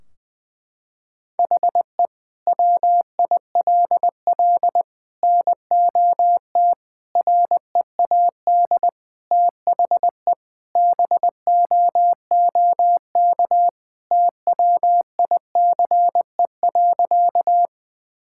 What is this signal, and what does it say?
Morse code, heard as a keyed tone, about 20 wpm, 700 Hz. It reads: HE WILL NOT READ THE BOOK TWICE.
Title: Sentences from Top 1600 Words 20wpm